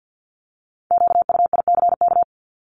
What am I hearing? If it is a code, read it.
Morse code, heard as a keyed tone, about 50 wpm, 700 Hz. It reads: QUICK
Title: Multi-letter Phonemes Repeated 3x 50wpm